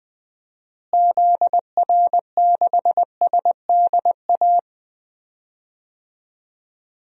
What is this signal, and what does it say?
Morse code, heard as a keyed tone, about 21 wpm, 700 Hz. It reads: ZR6SDA
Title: DXCC-100 Call Signs - 6 Characters - Repeated 3x - 20wpm